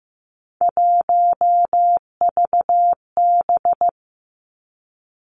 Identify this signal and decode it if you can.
Morse code, heard as a keyed tone, about 15 wpm, 700 Hz. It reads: 1VB
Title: Three Challenging Letter-Number Combinations 15wpm